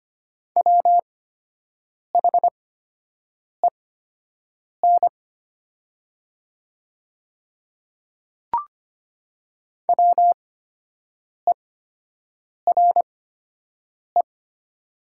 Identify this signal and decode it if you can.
Morse code, heard as a keyed tone, about 25 wpm, spaced as for 7 wpm, 700 Hz. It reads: WHEN WERE
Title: 8x Character Spacing - Top 100 Words - 25wpm